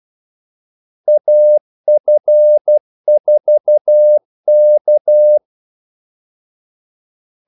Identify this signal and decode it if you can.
Morse code, heard as a keyed tone, about 12 wpm, 600 Hz. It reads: AF4K
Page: 2 ekstra ordmellemrum mellem kaldesignalerne